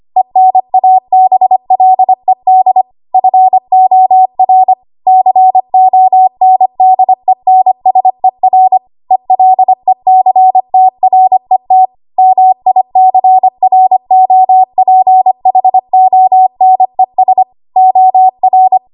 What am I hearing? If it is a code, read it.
Morse code, heard as a keyed tone, about 25 wpm, 750 Hz. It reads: ENABLED FOR CONDENSER ELECTRET MICROPHONES OR
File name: Archive / 25 WPM / 25WPM